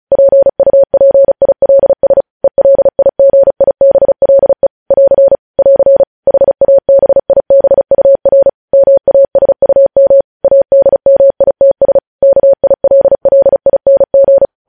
Morse code, heard as a keyed tone, about 36 wpm, 550 Hz. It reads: PUPILS ELIGIBLE <AR> <AR> HABIBUR MASUM ADMITS KILLING
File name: podcast / News headlines-1x-35x35